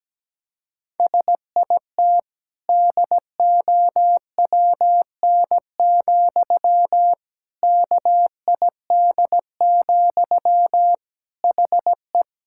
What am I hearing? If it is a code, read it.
Morse code, heard as a keyed tone, about 17 wpm, 700 Hz. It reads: SIT DOWN, KID, HE
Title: Sentences from Top 1600 Words 17wpm